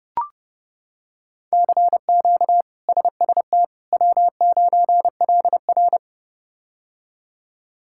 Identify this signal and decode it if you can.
Morse code, heard as a keyed tone, about 30 wpm, 700 Hz. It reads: CQ SST W9LR
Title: K1USN SST QSO Practice - 30wpm